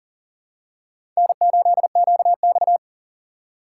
Morse code, heard as a keyed tone, about 40 wpm, 700 Hz. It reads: N8QX
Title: USA Call Signs - 4 Characters - 4x Character Spacing - 40wpm